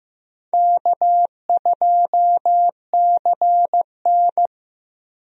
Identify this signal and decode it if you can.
Morse code, heard as a keyed tone, about 15 wpm, 700 Hz. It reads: K2CN
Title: USA Call Signs - 4 Characters - Repeated 2x - 15wpm